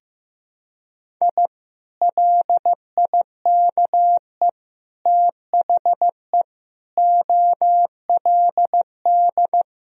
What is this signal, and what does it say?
Morse code, heard as a keyed tone, about 15 wpm, 700 Hz. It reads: I LIKE THE OLD
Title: Sentences from Top 1000 Words - Review Part 2 - 15wpm